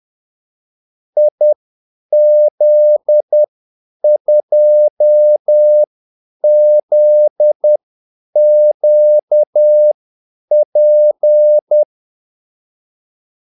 Morse code, heard as a keyed tone, about 10 wpm, spaced as for 8 wpm, 600 Hz. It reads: IZ2ZQP
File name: Callsigns-10-8wpm